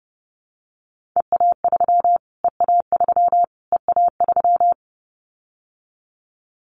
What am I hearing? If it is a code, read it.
Morse code, heard as a keyed tone, about 30 wpm, 700 Hz. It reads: EA3 EA3 EA3
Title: Callsign Prefixes for Top 25 DXCC - Repeated 3x - 30wpm